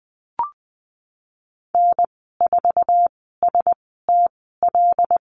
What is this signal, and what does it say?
Morse code, heard as a keyed tone, about 19 wpm, spaced as for 14 wpm, 700 Hz. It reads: N4STL
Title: 076 Lesson 13g - Letter L - Call Signs 20wpm